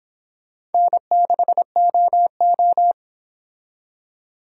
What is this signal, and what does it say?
Morse code, heard as a keyed tone, about 26 wpm, 700 Hz. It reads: N6OO